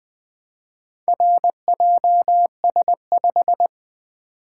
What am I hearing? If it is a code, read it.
Morse code, heard as a keyed tone, about 20 wpm, 700 Hz. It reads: RJS5
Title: Four Letter-Number Combinations - 20wpm